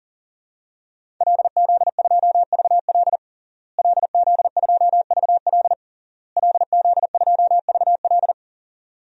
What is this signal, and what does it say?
Morse code, heard as a keyed tone, about 37 wpm, 700 Hz. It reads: LZ2VL LZ2VL LZ2VL
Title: DXCC-100 Call Signs - 5 Characters - Repeated 3x - 40wpm